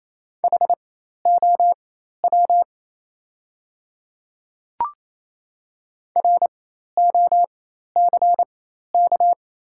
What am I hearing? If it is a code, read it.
Morse code, heard as a keyed tone, about 28 wpm, spaced as for 13 wpm, 690 Hz. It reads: HOW ROCK
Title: 4x Character Spacing - Top 500 Words - 28wpm